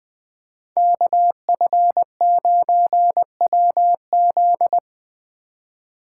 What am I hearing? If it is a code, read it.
Morse code, heard as a keyed tone, about 20 wpm, 700 Hz. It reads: KF9WZ
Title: USA Call Signs - 4x Character Spacing - 20wpm